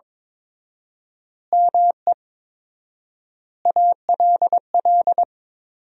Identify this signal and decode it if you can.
Morse code, heard as a keyed tone, about 21 wpm, 700 Hz. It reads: ME ALL
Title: 4x Word-Spacing - Sentences from Top 400 Words - 22wpm